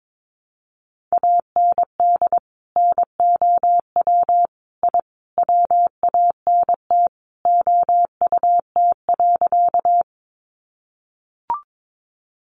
Morse code, heard as a keyed tone, about 22 wpm, 700 Hz. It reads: AND NOW I WANT OUT.
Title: A Sound Of Thunder 22wpm